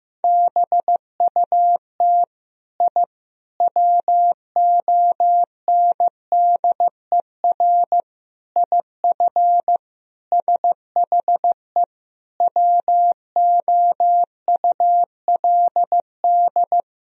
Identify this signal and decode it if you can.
Morse code, heard as a keyed tone, about 15 wpm, 700 Hz. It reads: BUT I WONDER IF SHE WOULD
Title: Sentences from Top 1000 Words - Review Part 3 - 15wpm